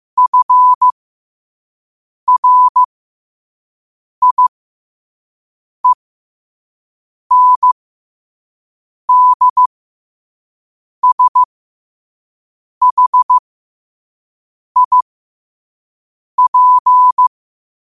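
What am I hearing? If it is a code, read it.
Morse code, heard as a keyed tone, about 15 wpm, spaced as for 5 wpm, 1000 Hz. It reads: FRIENDSHIP
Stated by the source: Morse code at 5 words per minute.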